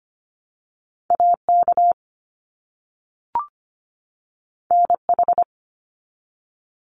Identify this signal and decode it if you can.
Morse code, heard as a keyed tone, about 25 wpm, 700 Hz. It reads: AK NH
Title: US States 25wpm